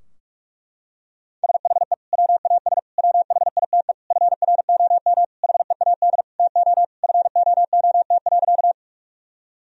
Morse code, heard as a keyed tone, about 47 wpm, 700 Hz. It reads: SHE WAS WHITE FROM HEAD TO FOOT.
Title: Sentences from Top 1000 Words - Review Part 3 - 45wpm